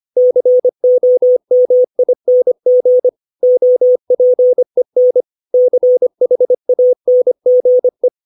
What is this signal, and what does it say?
Morse code, heard as a keyed tone, about 25 wpm, 500 Hz. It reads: COMING OPEN CHANGE